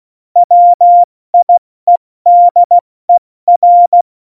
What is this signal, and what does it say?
Morse code, heard as a keyed tone, about 16 wpm, 700 Hz. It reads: WIEDER
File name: Morsekurs / Witze / Witz 01. Dezember 2020, 70 F80